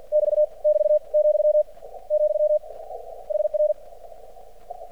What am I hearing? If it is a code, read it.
Morse code, heard as a keyed tone, about 50 wpm, 600 Hz. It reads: <BT> <BT> , , HR
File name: archived / CRSnet